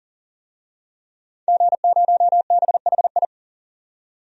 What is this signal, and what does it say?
Morse code, heard as a keyed tone, about 37 wpm, 700 Hz. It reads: G0BHI
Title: DXCC-100 Call Signs - 5 Characters - 40wpm